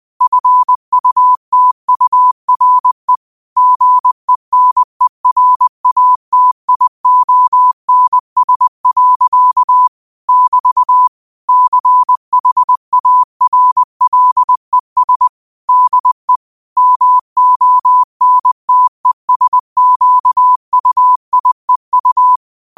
Morse code, heard as a keyed tone, about 20 wpm, 1000 Hz. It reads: FUTURE GENERATIONS. <BT> CHARLES DE MONTESQUIEU